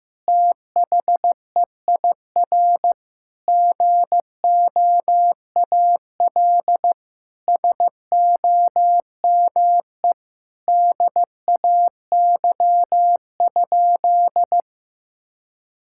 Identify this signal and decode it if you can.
Morse code, heard as a keyed tone, about 15 wpm, 700 Hz. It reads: THEIR GOAL SOME DAY?
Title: Sentences from Top 1000 Words - Review Part 6 - 15wpm